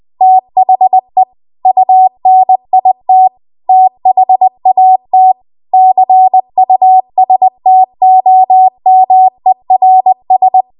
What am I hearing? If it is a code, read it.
Morse code, heard as a keyed tone, about 20 wpm, 750 Hz. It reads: THE UNIT THAT CUSTOMERS
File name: Archive / 20 WPM / 20WPM